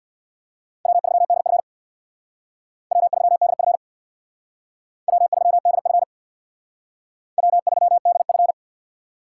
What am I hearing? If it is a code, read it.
Morse code, heard as a keyed tone, about 63 wpm, 690 Hz. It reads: W3DF W3DF W3DF W3DF